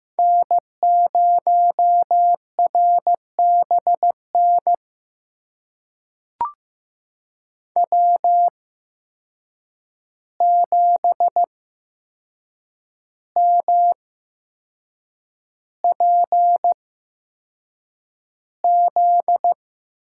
Morse code, heard as a keyed tone, about 15 wpm, spaced as for 6 wpm, 700 Hz. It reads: N0RBN W7MPZ
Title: USA Call Signs - 5 Characters - 8x Character Spacing - 15wpm